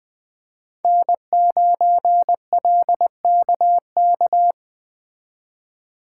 Morse code, heard as a keyed tone, about 22 wpm, 700 Hz. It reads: N9LKK